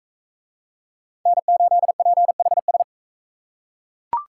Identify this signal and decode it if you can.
Morse code, heard as a keyed tone, about 39 wpm, 700 Hz. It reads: N8PHS